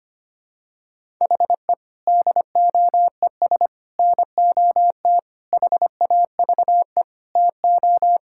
Morse code, heard as a keyed tone, about 25 wpm, 700 Hz. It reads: HE DOES NOT HAVE TO
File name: practice / 25wpm / Sentences with 4-Word Phrases - 25wpm